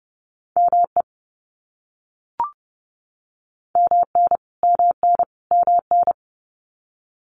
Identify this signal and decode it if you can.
Morse code, heard as a keyed tone, about 30 wpm, 700 Hz. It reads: ME MN MN MN
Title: US States Repeated 3x 30wpm